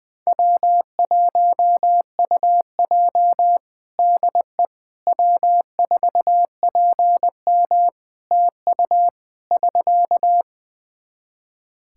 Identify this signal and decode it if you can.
Morse code, heard as a keyed tone, about 20 wpm, 700 Hz. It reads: W1UJ DE W4PM TU <SK>
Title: K1USN SST QSO Practice 20wpm